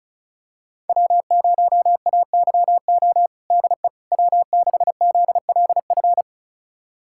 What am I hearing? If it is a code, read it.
Morse code, heard as a keyed tone, about 36 wpm, 700 Hz. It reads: W0AYO DE W6ZLF
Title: USA Callsign DE Callsign - 5 Characters - 35wpm